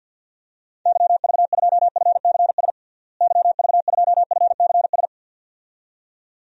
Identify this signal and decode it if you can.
Morse code, heard as a keyed tone, about 47 wpm, 700 Hz. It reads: YV2FCS YV2FCS
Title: DXCC-100 Call Signs - 6 Characters - Repeated 2x - 50wpm